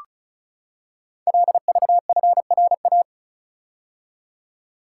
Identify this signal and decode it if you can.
Morse code, heard as a keyed tone, about 36 wpm, 700 Hz. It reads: LVFRA
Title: Code Groups - 5 Letters - 35wpm